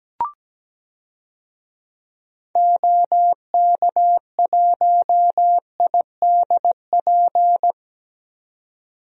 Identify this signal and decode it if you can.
Morse code, heard as a keyed tone, about 17 wpm, 700 Hz. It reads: OK1IDP
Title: DXCC-100 Call Signs - 6 Characters - 17wpm